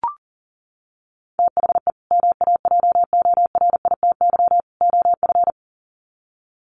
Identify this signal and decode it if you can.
Morse code, heard as a keyed tone, about 37 wpm, 700 Hz. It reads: THE MAJORITY OF
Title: Sets of 3 Words 40wpm